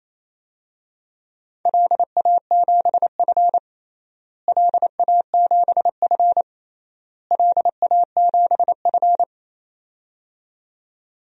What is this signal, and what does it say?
Morse code, heard as a keyed tone, about 28 wpm, 700 Hz. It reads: LA7F LA7F LA7F
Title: DXCC-100 Call Signs - 4 Characters - Repeated 3x - 28wpm